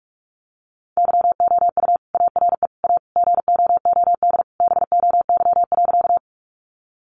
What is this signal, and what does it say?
Morse code, heard as a keyed tone, about 44 wpm, 700 Hz. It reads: YOU ARE A GOOD BOY.
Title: Sentences from Top 200 Words 45wpm